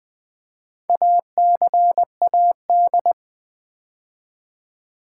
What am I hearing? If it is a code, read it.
Morse code, heard as a keyed tone, about 20 wpm, 700 Hz. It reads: ACAD
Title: Build A Word - 2 Letters At A Time - 8-Letter Words - 20wpm